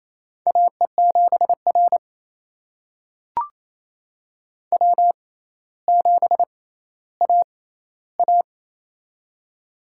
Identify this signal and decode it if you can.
Morse code, heard as a keyed tone, about 28 wpm, spaced as for 14 wpm, 700 Hz. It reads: AE7R W7AA
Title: USA Call Signs - 6x Character Spacing - 28wpm